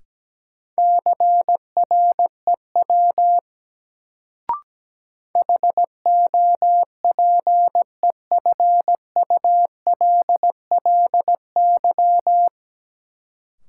Morse code, heard as a keyed tone, about 17 wpm, 700 Hz. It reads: CREW HOPEFULLY
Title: Sentences from Top 1500 Words - New Word Drill Alt - 17wpm